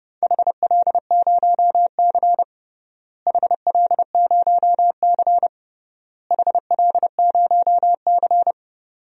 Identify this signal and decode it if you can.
Morse code, heard as a keyed tone, about 30 wpm, 700 Hz. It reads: HL0C HL0C HL0C
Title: DXCC-100 Call Signs - 4 Characters - Repeated 3x - 30wpm